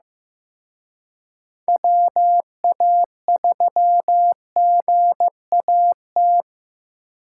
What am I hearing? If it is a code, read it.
Morse code, heard as a keyed tone, about 18 wpm, 700 Hz. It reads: WA3GAT
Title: Speed Racing - USA 2x3 Call Signs 15wpm